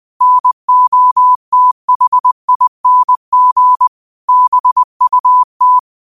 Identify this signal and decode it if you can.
Morse code, heard as a keyed tone, about 20 wpm, 1000 Hz. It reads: NOTHING BUT
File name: qotdfiles / 20wpm